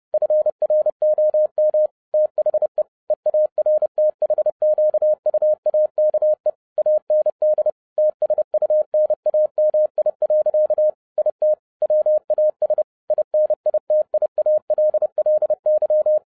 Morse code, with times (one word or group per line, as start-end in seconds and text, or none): 0.1-1.9 FROM
2.1-2.8 THE
3.1-6.5 EARTHQUAKE
6.8-7.7 AND
8.0-10.9 TSUNAMI.
11.2-11.5 IT
11.8-12.8 WAS
13.1-16.2 INITIALLY